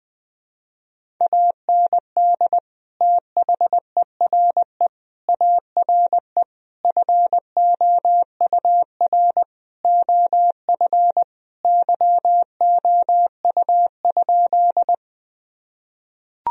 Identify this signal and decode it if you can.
Morse code, heard as a keyed tone, about 20 wpm, 700 Hz. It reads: AND THERE ARE FOUR OF YOU?
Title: Sentences from Top 1000 Words - Review Part 1 - 20wpm